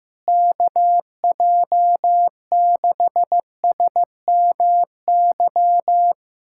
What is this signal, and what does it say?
Morse code, heard as a keyed tone, about 15 wpm, 700 Hz. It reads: KJ6SMY
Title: USA Callsign DE Callsign - 6 Characters - 15wpm